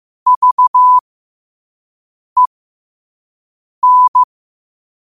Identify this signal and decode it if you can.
Morse code, heard as a keyed tone, about 15 wpm, spaced as for 5 wpm, 1000 Hz. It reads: VEN
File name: qotdfiles / 5wpm